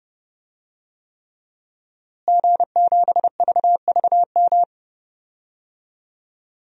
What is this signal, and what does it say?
Morse code, heard as a keyed tone, about 30 wpm, 700 Hz. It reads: G7VVM
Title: DXCC-100 Call Signs - 5 Characters - Repeated 3x - 30wpm